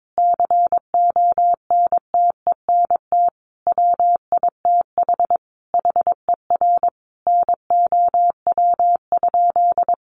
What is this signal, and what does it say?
Morse code, heard as a keyed tone, about 22 wpm, 700 Hz. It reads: CONTENT WITH HER NOW?
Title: Sentences from Top 900 Words 22wpm